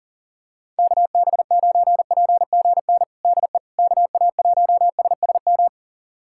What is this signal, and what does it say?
Morse code, heard as a keyed tone, about 39 wpm, 700 Hz. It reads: KB9PGN DE KA1SSM
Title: USA Callsign DE Callsign - 6 Characters - 40wpm